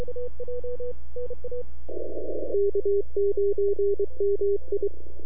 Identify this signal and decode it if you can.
Morse code, heard as a keyed tone, about 23 wpm, 400 Hz. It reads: K9MI
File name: NAQPCW-8-2009 / W1UJ NAQP CW 09 2300Z